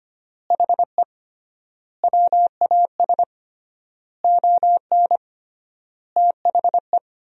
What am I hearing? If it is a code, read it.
Morse code, heard as a keyed tone, about 25 wpm, 700 Hz. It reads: HE WAS ON THE
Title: 3x Word-Spacing - Sentences from Top 500 Words - 25wpm